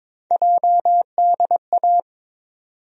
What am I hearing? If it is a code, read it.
Morse code, heard as a keyed tone, about 22 wpm, 700 Hz. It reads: JDA
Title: USA Call Sign Postfixes - 3 Letters 22wpm